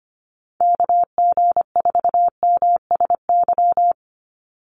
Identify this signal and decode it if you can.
Morse code, heard as a keyed tone, about 25 wpm, 700 Hz. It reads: KG4MSY
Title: USA Call Signs 25wpm